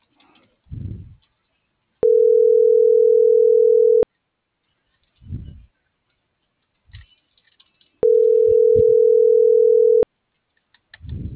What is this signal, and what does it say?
Ringback tone (North America), 440 + 480 Hz.